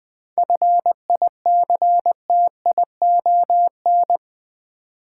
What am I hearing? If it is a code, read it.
Morse code, heard as a keyed tone, about 20 wpm, 700 Hz. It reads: FICTION
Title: 7 Letter Words - First 3 Letters Given - 20wpm